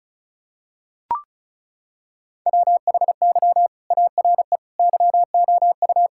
Morse code, heard as a keyed tone, about 36 wpm, 700 Hz. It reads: WHY ARE YOU
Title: Sentences from Top 1000 Words - Review Part 5 - 35wpm